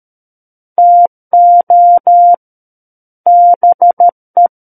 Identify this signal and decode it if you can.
Morse code, heard as a keyed tone, about 13 wpm, 700 Hz. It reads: TO BE
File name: Book One / 04-Chapter 4 - 13 WPM